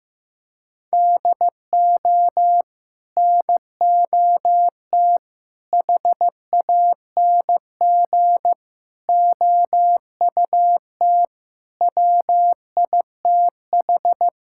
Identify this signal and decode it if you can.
Morse code, heard as a keyed tone, about 15 wpm, 700 Hz. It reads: DO NOT HANG OUT WITH
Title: Sentences from Top 1600 Words 15wpm